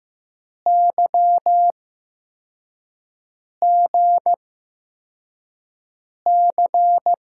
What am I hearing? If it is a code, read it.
Morse code, heard as a keyed tone, about 15 wpm, spaced as for 4 wpm, 700 Hz. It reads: YGC
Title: USA Call Signs - 6 Characters - 8x Character Spacing - 15wpm